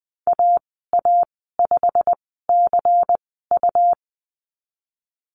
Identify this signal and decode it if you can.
Morse code, heard as a keyed tone, about 19 wpm, spaced as for 14 wpm, 700 Hz. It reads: AA5CU
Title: 140 Lesson 23g - Letter f - Call Signs 20wpm